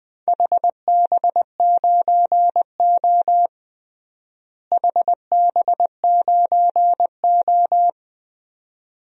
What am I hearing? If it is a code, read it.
Morse code, heard as a keyed tone, about 20 wpm, 700 Hz. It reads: HB9O HB9O
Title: DXCC-100 Call Signs - 4 Characters - Repeated 2x - 20wpm